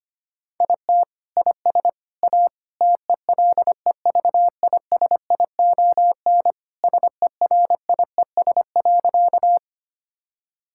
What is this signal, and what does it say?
Morse code, heard as a keyed tone, about 25 wpm, 700 Hz. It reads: IT IS A TELEVISION SERIES.
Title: Sentences from Top 1000 Words - Review Part 5 - 25wpm